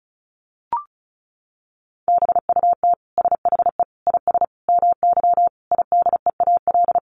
Morse code, heard as a keyed tone, about 34 wpm, 700 Hz. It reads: BUT SHE IS MY IDEAL
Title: Sentences from Top 1400 Words 35wpm